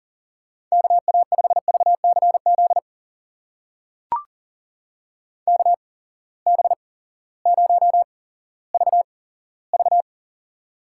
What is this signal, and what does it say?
Morse code, heard as a keyed tone, about 39 wpm, spaced as for 18 wpm, 700 Hz. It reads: KA5VCZ KB0VV
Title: USA Call Signs - 8x Character Spacing - 40wpm